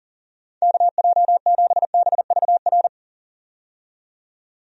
Morse code, heard as a keyed tone, about 37 wpm, 700 Hz. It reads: KJ7BVR
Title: USA Call Signs - 6 Characters - 40wpm